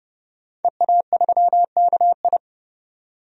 Morse code, heard as a keyed tone, about 31 wpm, 700 Hz. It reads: EA3KI
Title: DXCC-100 Call Signs - 5 Characters - 30wpm